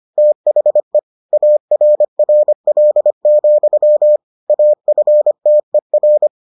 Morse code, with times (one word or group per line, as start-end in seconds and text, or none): 0.2-1.0 THE
1.3-4.2 ARRL,
4.5-6.3 AFTER